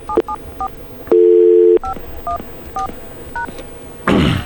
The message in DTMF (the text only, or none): 774 514#